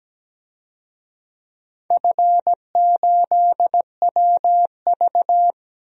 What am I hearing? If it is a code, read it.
Morse code, heard as a keyed tone, about 17 wpm, 700 Hz. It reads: F8WV
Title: DXCC-100 Call Signs - 4 Characters - Repeated 3x - 17wpm